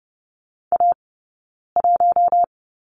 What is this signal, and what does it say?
Morse code, heard as a keyed tone, about 30 wpm, 700 Hz. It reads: A 1
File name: Lesson 20d - Number 3 - Two Characters 30wpm